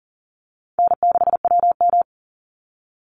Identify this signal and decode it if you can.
Morse code, heard as a keyed tone, about 40 wpm, 700 Hz. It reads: N6WM
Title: Easy Call Signs Repeated 2x - 40wpm